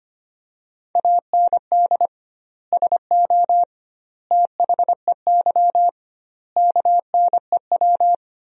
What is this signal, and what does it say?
Morse code, heard as a keyed tone, about 25 wpm, 700 Hz. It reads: AND SO THEY KNEW
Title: 2x Word-Spacing - Sentences from Top 400 Words - 25wpm